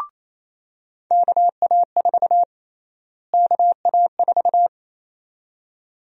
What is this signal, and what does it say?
Morse code, heard as a keyed tone, about 28 wpm, 700 Hz. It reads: KA4 KA4
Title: USA Call Sign Prefixes - Repeated 2x - 28wpm